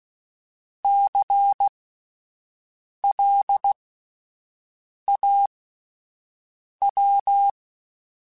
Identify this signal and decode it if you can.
Morse code, heard as a keyed tone, about 16 wpm, spaced as for 5 wpm, 800 Hz. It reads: CLAW
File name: Alphabetical / Alphabetical mp3s / Cs @5